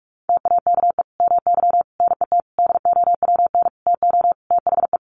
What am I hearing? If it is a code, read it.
Morse code, heard as a keyed tone, about 44 wpm, 700 Hz. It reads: TAKE MY NET DOWN TO THE
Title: Sentences from Top 1400 Words 45wpm